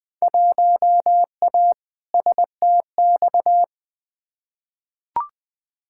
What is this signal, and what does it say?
Morse code, heard as a keyed tone, about 20 wpm, 700 Hz. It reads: 1A STX
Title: Field Day Exchanges Encore - 20wpm